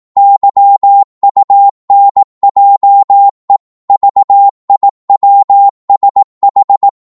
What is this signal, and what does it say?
Morse code, heard as a keyed tone, about 18 wpm, 800 Hz. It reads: YUNJE VIWSH
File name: Morse / ltr-18wpm